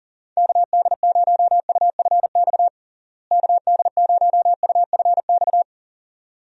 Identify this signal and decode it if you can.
Morse code, heard as a keyed tone, about 37 wpm, 700 Hz. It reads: KD0UFX KD0UFX
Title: USA Call Signs - 6 Characters - Repeated 2x - 40wpm